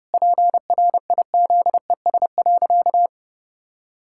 Morse code, heard as a keyed tone, about 30 wpm, 700 Hz. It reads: PRIZES.
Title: Pangram Challenge v2 - 30wpm